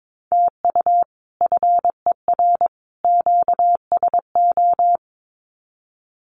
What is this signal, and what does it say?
Morse code, heard as a keyed tone, about 22 wpm, 700 Hz. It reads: TU FER QSO
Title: Common QSO Phrases 22wpm